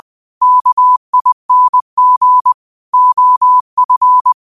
Morse code, heard as a keyed tone, about 20 wpm, 1000 Hz. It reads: KING OF
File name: qotdfiles / 20wpm